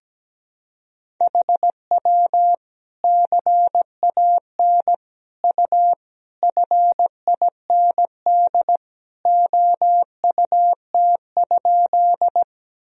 Morse code, heard as a keyed tone, about 17 wpm, 700 Hz. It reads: HW CAN U FIND OUT?
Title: Sentences from Top 200 Words and Abbreviations 17wpm